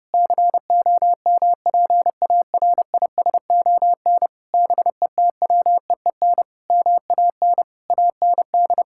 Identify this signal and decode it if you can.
Morse code, heard as a keyed tone, about 30 wpm, 700 Hz. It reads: COMPARISON BETWEEN MAN AND
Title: Sentences from Top 1600 Words 30wpm